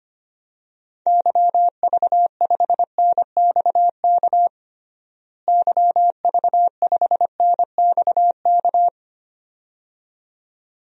Morse code, heard as a keyed tone, about 25 wpm, 700 Hz. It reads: YV5NXK YV5NXK
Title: DXCC-100 Call Signs - 6 Characters - Repeated 2x - 25wpm